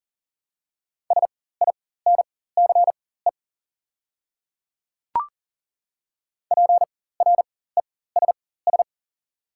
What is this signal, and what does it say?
Morse code, heard as a keyed tone, about 41 wpm, spaced as for 19 wpm, 700 Hz. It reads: SINCE PRESS
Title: 4x Character Spacing - Top 300 Words - 40wpm